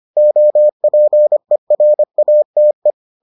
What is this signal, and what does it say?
Morse code, heard as a keyed tone, about 25 wpm, 600 Hz. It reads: OPERATE